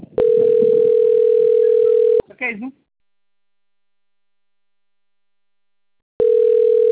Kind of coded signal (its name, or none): telephone call progress tone